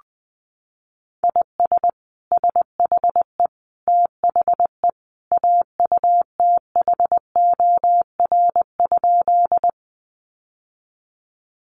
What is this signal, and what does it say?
Morse code, heard as a keyed tone, about 20 wpm, 700 Hz. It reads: IS SHE THE AUTHOR?
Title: Sentences from Top 1200 Words 20wpm